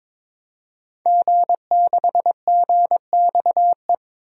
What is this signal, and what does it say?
Morse code, heard as a keyed tone, about 22 wpm, 700 Hz. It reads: G6GXE
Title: DXCC-100 Call Signs - 5 Characters - 22wpm